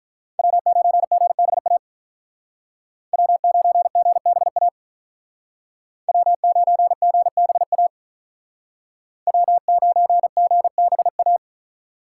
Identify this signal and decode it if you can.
Morse code, heard as a keyed tone, about 43 wpm, 700 Hz. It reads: W9GBA W9GBA W9GBA W9GBA
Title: Speed Racing - USA 1x3 Call Signs 35wpm